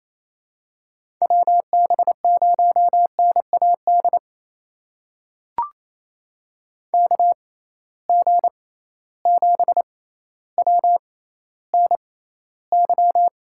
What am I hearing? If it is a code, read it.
Morse code, heard as a keyed tone, about 28 wpm, spaced as for 14 wpm, 700 Hz. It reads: WB0NAD KG7WNY
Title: USA Call Signs - 6 Characters - 6x Character Spacing - 28wpm